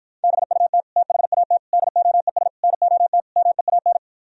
Morse code, heard as a keyed tone, about 52 wpm, 700 Hz. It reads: BUT THAT DOES NOT MEAN